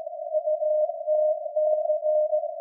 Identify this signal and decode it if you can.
Morse code, heard as a keyed tone, about 17 wpm, 610 Hz. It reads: UTC
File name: iz7slz / VLF / SAQ jun 2019